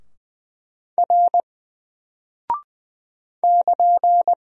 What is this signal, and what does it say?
Morse code, heard as a keyed tone, about 20 wpm, 700 Hz. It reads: R <KN>
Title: Single Character (Letter:Number:Punctuation:Prosign) 20wpm